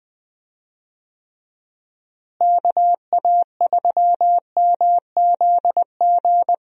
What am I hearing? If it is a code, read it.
Morse code, heard as a keyed tone, about 20 wpm, 700 Hz. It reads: KA3MZG
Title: USA Call Signs - 6 Characters - Repeated 3x - 20wpm